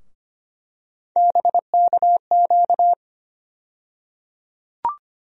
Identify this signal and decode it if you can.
Morse code, heard as a keyed tone, about 25 wpm, 700 Hz. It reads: BKQ
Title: USA Call Sign Postfixes - 3 Letters 25wpm